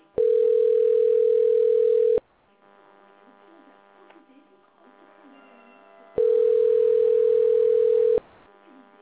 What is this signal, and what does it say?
Ringback tone (North America), 440 + 480 Hz.